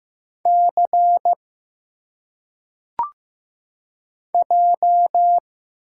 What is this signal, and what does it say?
Morse code, heard as a keyed tone, about 15 wpm, 700 Hz. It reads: C J
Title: Single Letters - Phonetic Pronunciation 15wpm